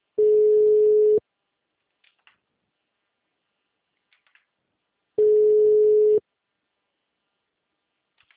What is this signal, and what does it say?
Ringback tone (Europe, 425 Hz).
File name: out- -2011- - - .77806